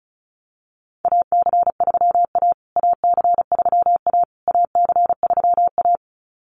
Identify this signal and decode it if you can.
Morse code, heard as a keyed tone, about 34 wpm, 700 Hz. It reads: AC3A AC3A AC3A
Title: USA Call Signs Repeated 3x 35wpm